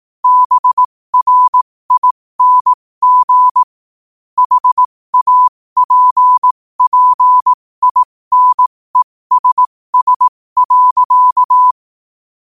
Morse code, heard as a keyed tone, about 18 wpm, spaced as for 15 wpm, 1000 Hz. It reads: BRING HAPPINESS.
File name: qotdfiles / 15wpm